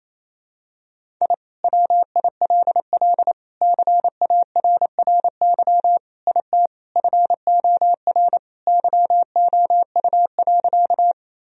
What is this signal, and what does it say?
Morse code, heard as a keyed tone, about 27 wpm, 700 Hz. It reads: I WILL CARRY IT FOR YOU.
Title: Sentences from Top 1000 Words - Review Part 1 - 28wpm